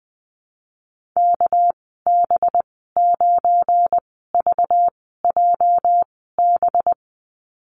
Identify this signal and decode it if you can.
Morse code, heard as a keyed tone, about 20 wpm, spaced as for 14 wpm, 700 Hz. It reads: KB9VJB
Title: 219 Lesson 35g - Number 0 - Call Signs 20wpm